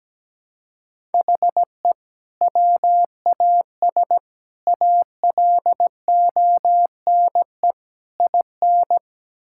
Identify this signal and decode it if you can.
Morse code, heard as a keyed tone, about 17 wpm, 700 Hz. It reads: HE WAS ALONE IN
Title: Sentences from Top 1000 Words - Review Part 6 - 17wpm